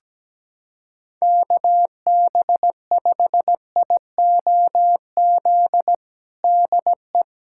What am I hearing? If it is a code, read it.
Morse code, heard as a keyed tone, about 17 wpm, 700 Hz. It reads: KB5IOZ DE
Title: USA Callsign DE Callsign 17wpm